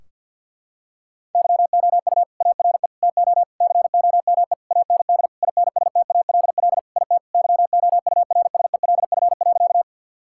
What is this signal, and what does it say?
Morse code, heard as a keyed tone, about 51 wpm, 700 Hz. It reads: YOU ARE TO COME AND INSTALL IT YOURSELF.
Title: Sentences from Top 1600 Words 50wpm